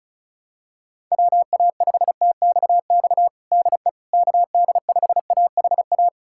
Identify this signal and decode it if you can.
Morse code, heard as a keyed tone, about 36 wpm, 700 Hz. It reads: WA5TXX DE KD5AHA
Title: USA Callsign DE Callsign - 6 Characters - 35wpm